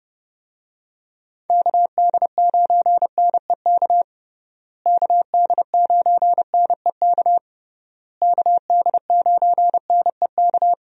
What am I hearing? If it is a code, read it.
Morse code, heard as a keyed tone, about 32 wpm, 700 Hz. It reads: KD9NEK KD9NEK KD9NEK
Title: USA Call Signs - 6 Characters - Repeated 3x - 30wpm